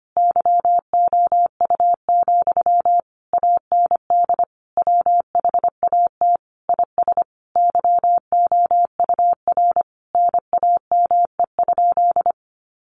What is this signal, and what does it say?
Morse code, heard as a keyed tone, about 25 wpm, 700 Hz. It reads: YOU, AND WHAT IS YOUR NAME?
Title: Sentences from Top 1200 Words 25wpm